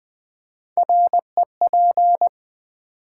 Morse code, heard as a keyed tone, about 20 wpm, 700 Hz. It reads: REP
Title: Build A Word - 3 Letters At A Time - 9-Letter Words - 20wpm